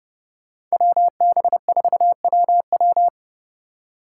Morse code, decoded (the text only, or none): WB4WW